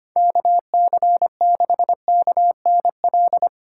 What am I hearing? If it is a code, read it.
Morse code, heard as a keyed tone, about 25 wpm, 700 Hz. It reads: KC6KNL
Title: USA Call Signs - 6 Characters - 2x Character Spacing - 25wpm